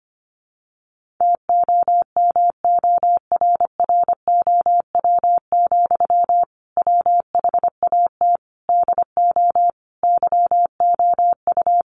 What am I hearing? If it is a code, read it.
Morse code, heard as a keyed tone, about 25 wpm, 700 Hz. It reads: TOMORROW, WHAT DO YOU
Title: Sentences from Top 1500 Words 25wpm